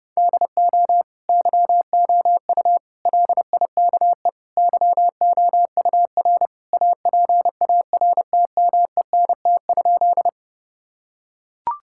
Morse code, heard as a keyed tone, about 32 wpm, 700 Hz. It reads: DO YOU LIKE YOUR APARTMENT?
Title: Sentences from Top 1000 Words - Review Part 6 - 30wpm